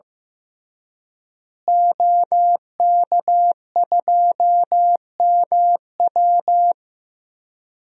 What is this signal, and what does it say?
Morse code, heard as a keyed tone, about 15 wpm, 700 Hz. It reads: OK2MW
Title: DXCC-100 Call Signs - 5 Characters - Repeated 3x - 15wpm